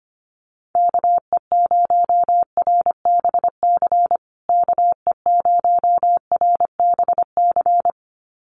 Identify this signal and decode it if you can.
Morse code, heard as a keyed tone, about 25 wpm, 700 Hz. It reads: KE0RBC KE0RBC
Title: USA Call Signs Repeated 2x 25wpm